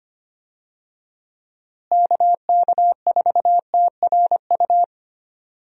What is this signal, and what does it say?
Morse code, heard as a keyed tone, about 25 wpm, 700 Hz. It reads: KK4TRU
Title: USA Call Signs - 6 Characters - Repeated 3x - 25wpm